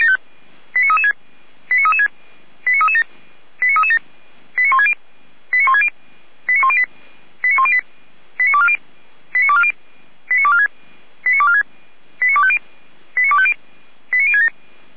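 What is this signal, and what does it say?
Five-tone selective call (ZVEI), heard as quick runs of five tones, tones of about 70 ms. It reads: 89336 89336 89337 89337 89160 89160 89108 89108 89240 89240 89246 89246 89250 89250 89067
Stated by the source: ZVEI five-tone selective call received on an SDR.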